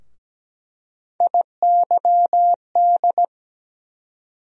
Morse code, heard as a keyed tone, about 17 wpm, 700 Hz. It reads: IYD
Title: USA Call Sign Postfixes - 3 Letters 17wpm